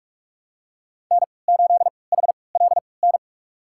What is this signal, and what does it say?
Morse code, heard as a keyed tone, about 46 wpm, spaced as for 26 wpm, 700 Hz. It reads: N8HLN
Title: USA Call Signs - 5 Characters - 3x Character Spacing - 45wpm